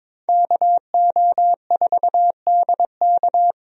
Morse code, heard as a keyed tone, about 22 wpm, 700 Hz. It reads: KO4DK
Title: USA Call Signs - 5 Characters - 22wpm